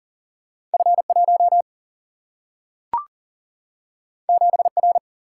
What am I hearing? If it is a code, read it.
Morse code, heard as a keyed tone, about 40 wpm, 700 Hz. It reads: F 1 7 R
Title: Single-Letter Single-Number 40wpm